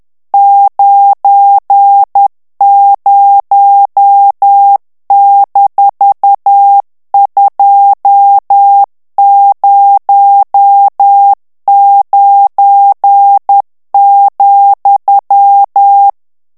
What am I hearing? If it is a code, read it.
Morse code, heard as a keyed tone, about 10 wpm, 800 Hz. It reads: 90-209,
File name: ANGLAIS texte clair 3 vitesse 10 mots minute